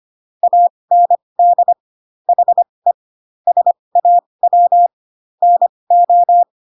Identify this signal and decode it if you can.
Morse code, heard as a keyed tone, about 25 wpm, spaced as for 20 wpm, 700 Hz. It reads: AND HE SAW NO
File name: Pinocchio / Pinocchio ch1 15 25 ik1hge